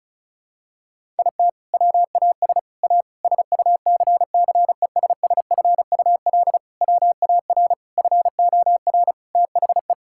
Morse code, heard as a keyed tone, about 34 wpm, 700 Hz. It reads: IT WAS A SUCCESSFUL WAR FOR THE
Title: Sentences from Top 1000 Words - Review Part 6 - 35wpm